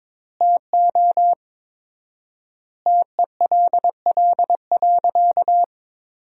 Morse code, heard as a keyed tone, about 22 wpm, 700 Hz. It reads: TO TELL.
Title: 4x Word-Spacing - Sentences from Top 200 Words - 22wpm